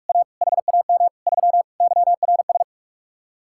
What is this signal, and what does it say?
Morse code, heard as a keyed tone, about 49 wpm, 700 Hz. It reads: A HAM 3 YRS